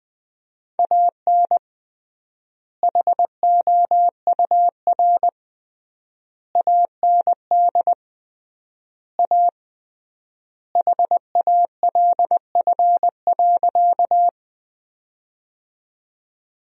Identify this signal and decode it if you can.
Morse code, heard as a keyed tone, about 20 wpm, 700 Hz. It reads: AN HOUR AND A HALF.
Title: 3x Word-Spacing - Sentences from Top 400 Words - 20wpm